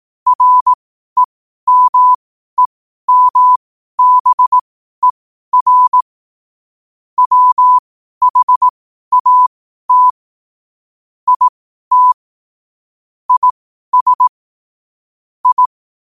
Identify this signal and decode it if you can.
Morse code, heard as a keyed tone, about 18 wpm, spaced as for 12 wpm, 1000 Hz. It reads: REMEMBER WHAT IT IS I
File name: qotdfiles / 12wpm